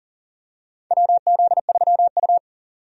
Morse code, heard as a keyed tone, about 40 wpm, 700 Hz. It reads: WZ3U
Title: USA Call Signs - 4 Characters - 40wpm